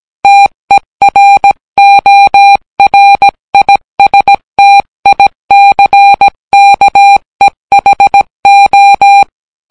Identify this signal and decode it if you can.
Morse code, heard as a keyed tone, about 17 wpm, 800 Hz. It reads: TERORISTICKEHO